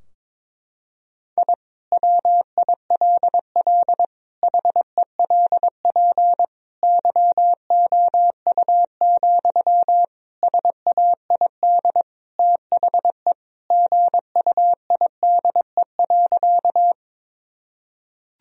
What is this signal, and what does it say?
Morse code, heard as a keyed tone, about 22 wpm, 700 Hz. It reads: I WILL HELP YOU, SAID THE GUIDE.
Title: Sentences from Top 1000 Words - Review Part 3 - 22wpm